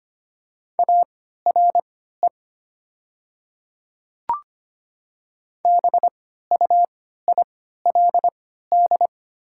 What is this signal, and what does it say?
Morse code, heard as a keyed tone, about 25 wpm, spaced as for 14 wpm, 700 Hz. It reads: ARE BUILD
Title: 3x Character Spacing - Top 200 Words - 25wpm